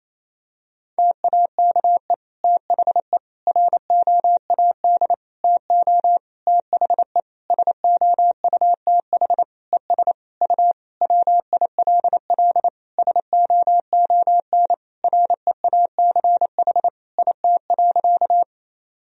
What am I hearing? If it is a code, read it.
Morse code, heard as a keyed tone, about 28 wpm, 700 Hz. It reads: TAKE THE ROAD TO THE SOUTH ES U WILL SOON REACH IT.
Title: Sentences from Top 400 Words and Abbreviations 28wpm